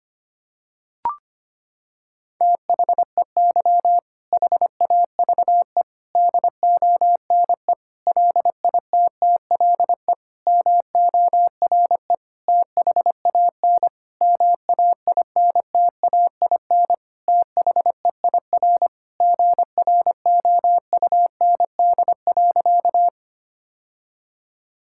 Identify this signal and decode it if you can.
Morse code, heard as a keyed tone, about 25 wpm, 700 Hz. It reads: THEY HAVE DONE LITTLE MORE THAN MAINTAIN THEIR GROUND.
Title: Sentences from Top 1000 Words - Review Part 5 - 25wpm